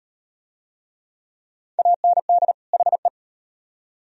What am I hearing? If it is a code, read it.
Morse code, heard as a keyed tone, about 35 wpm, 700 Hz. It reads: AND HE